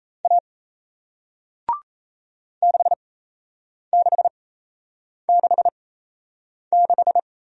Morse code, heard as a keyed tone, about 37 wpm, spaced as for 8 wpm, 700 Hz. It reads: A 6666